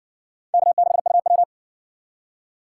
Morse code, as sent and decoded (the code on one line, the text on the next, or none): -.- -.... ..- .--
K6UW